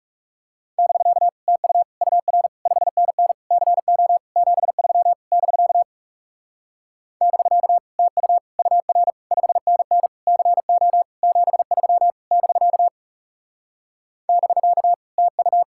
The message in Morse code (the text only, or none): <BK> TU UR 5NN CO 73 <BK> <BK> TU UR 5NN CO 73 <BK> <BK> TU